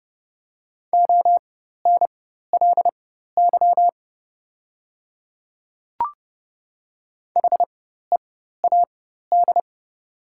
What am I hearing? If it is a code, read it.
Morse code, heard as a keyed tone, about 32 wpm, spaced as for 14 wpm, 700 Hz. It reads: ONLY HEAD
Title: 4x Character Spacing - Top 300 Words - 30wpm